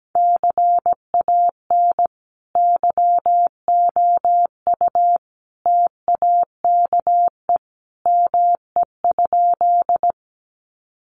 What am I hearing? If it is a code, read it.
Morse code, heard as a keyed tone, about 17 wpm, 700 Hz. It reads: CAN YOU TAKE ME?
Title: Sentences from Top 200 Words 17wpm